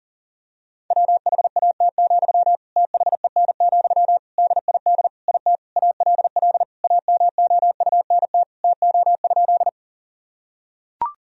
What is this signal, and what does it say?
Morse code, heard as a keyed tone, about 37 wpm, 700 Hz. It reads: WHAT, THEN, DID IT ALL AMOUNT TO?
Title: Sentences from Top 1000 Words - Review Part 4 - 40wpm